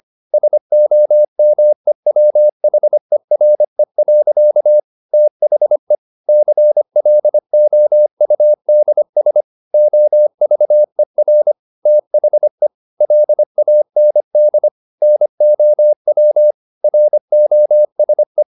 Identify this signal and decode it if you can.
Morse code, heard as a keyed tone, about 25 wpm, 600 Hz. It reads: SOMEWHERE. THE CLOUDS OVER THE LAND NOW ROSE
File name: the old man and the sea part 1 25wpm